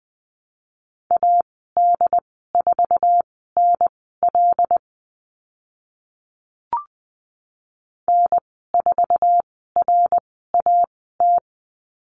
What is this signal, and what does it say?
Morse code, heard as a keyed tone, about 20 wpm, spaced as for 15 wpm, 700 Hz. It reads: AD4NL N4RAT
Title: 076 Lesson 13g - Letter L - Call Signs 20wpm